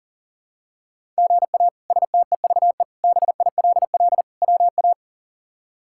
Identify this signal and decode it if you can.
Morse code, heard as a keyed tone, about 37 wpm, 700 Hz. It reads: GA STEVE BILL WA
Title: K1USN SST QSO Practice - 40wpm